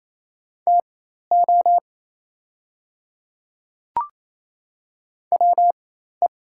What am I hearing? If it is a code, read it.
Morse code, heard as a keyed tone, about 28 wpm, spaced as for 13 wpm, 700 Hz. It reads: TO WE
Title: 4x Character Spacing - Top 300 Words - 28wpm